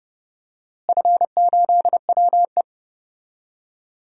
Morse code, heard as a keyed tone, about 30 wpm, 700 Hz. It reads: F8WE
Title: DXCC-100 Call Signs - 4 Characters - Repeated 2x - 30wpm